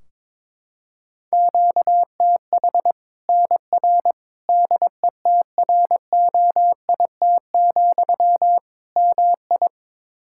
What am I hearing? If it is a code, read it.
Morse code, heard as a keyed tone, about 22 wpm, 700 Hz. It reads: QTH NR DETROIT, MI
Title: Practice QSO - Exchange 1 - Beginning - 22wpm